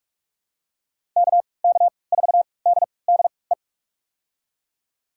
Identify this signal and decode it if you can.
Morse code, heard as a keyed tone, about 42 wpm, spaced as for 25 wpm, 700 Hz. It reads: KK4DDE
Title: USA Call Signs - 6 Characters - 3x Character Spacing - 45wpm